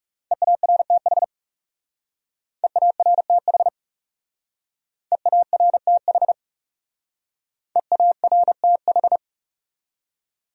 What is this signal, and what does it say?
Morse code, heard as a keyed tone, about 40 wpm, 700 Hz. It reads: EARTH EARTH EARTH EARTH